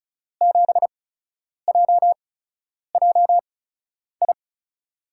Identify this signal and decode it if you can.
Morse code, heard as a keyed tone, about 36 wpm, spaced as for 10 wpm, 700 Hz. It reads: 7JJI
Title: USA Call Signs - 6 Characters - 8x Character Spacing - 35wpm